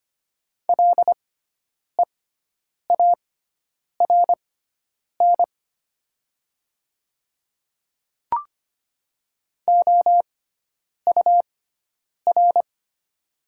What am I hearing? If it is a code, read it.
Morse code, heard as a keyed tone, about 25 wpm, spaced as for 9 wpm, 700 Hz. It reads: LEARN OUR
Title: 6x Character Spacing - Top 500 Words - 25wpm